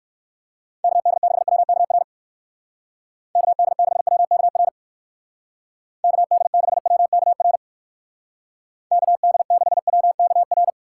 Morse code, heard as a keyed tone, about 61 wpm, 700 Hz. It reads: KD6WKR KD6WKR KD6WKR KD6WKR